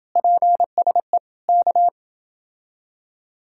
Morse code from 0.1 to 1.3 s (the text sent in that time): PSE